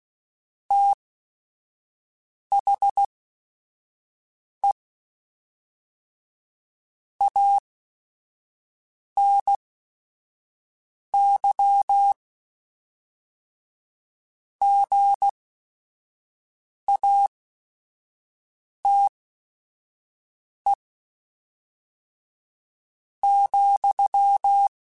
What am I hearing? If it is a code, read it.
Morse code, heard as a keyed tone, about 16 wpm, spaced as for 5 wpm, 800 Hz. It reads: THE ANY GATE ,